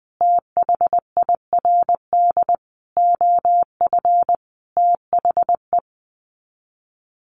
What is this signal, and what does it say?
Morse code, heard as a keyed tone, about 20 wpm, 700 Hz. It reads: THIRD OF THE
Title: Sets of 5 Words Encore2 20wpm